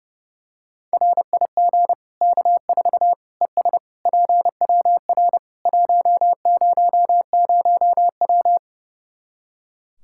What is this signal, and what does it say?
Morse code, heard as a keyed tone, about 30 wpm, 700 Hz. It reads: RIG K4 ES PWR 100W
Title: Practice QSO - Exchange 2 - Info - 30wpm